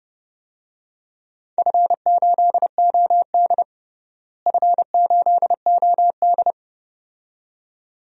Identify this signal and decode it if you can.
Morse code, heard as a keyed tone, about 32 wpm, 700 Hz. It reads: F8OD F8OD
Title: DXCC-100 Call Signs - 4 Characters - Repeated 2x - 30wpm